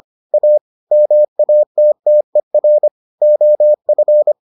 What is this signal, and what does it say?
Morse code, heard as a keyed tone, about 25 wpm, 600 Hz. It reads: A MATTER OF